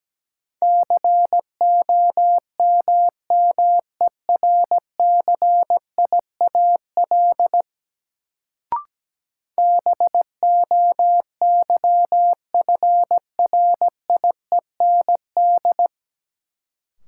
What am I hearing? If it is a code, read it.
Morse code, heard as a keyed tone, about 17 wpm, 700 Hz. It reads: COMMERCIAL BOYFRIEND
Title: Sentences from Top 1000 Words - Review Part 5 - New Word Drill - 17wpm